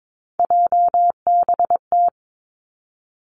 Morse code, heard as a keyed tone, about 22 wpm, 700 Hz. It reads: JBT
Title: Three Challenging Letter-Number Combinations 22wpm